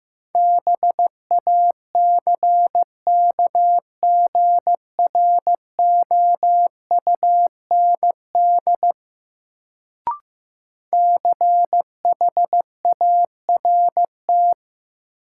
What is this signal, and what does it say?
Morse code, heard as a keyed tone, about 15 wpm, 700 Hz. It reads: BACKGROUND CHART
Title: Sentences from Top 1300 Words - New Word Drill - 15wpm